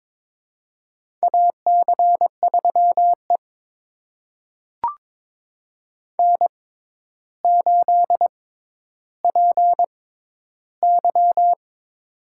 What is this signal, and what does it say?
Morse code, heard as a keyed tone, about 22 wpm, spaced as for 11 wpm, 700 Hz. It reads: AC3E N8PY
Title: USA Call Signs - 4 Characters - 6x Character Spacing - 22wpm